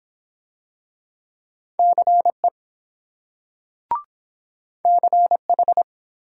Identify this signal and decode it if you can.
Morse code, heard as a keyed tone, about 23 wpm, 700 Hz. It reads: CE CH